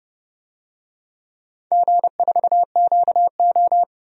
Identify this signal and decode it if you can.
Morse code, heard as a keyed tone, about 30 wpm, 700 Hz. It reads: G4QO
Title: DXCC-100 Call Signs - 4 Characters - 30wpm